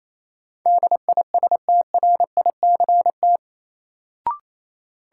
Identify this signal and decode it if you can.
Morse code, heard as a keyed tone, about 28 wpm, 700 Hz. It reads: DISTRICT
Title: Sentences from Top 1600 Words - New Word Drill Alt - 28wpm